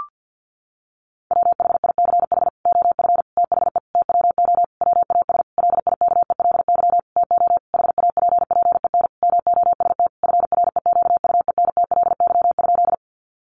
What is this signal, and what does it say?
Morse code, heard as a keyed tone, about 48 wpm, 700 Hz. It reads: WHICH OF THE TWO WAS LIKELY TO HAPPEN MOST FREQUENTLY?
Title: Sentences from Top 1300 Words 50wpm